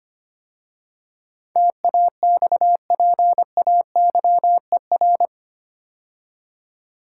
Morse code, heard as a keyed tone, about 25 wpm, 700 Hz. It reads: TAXPAYER
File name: 8 Letter Words - First 3 Letters Given - 25wpm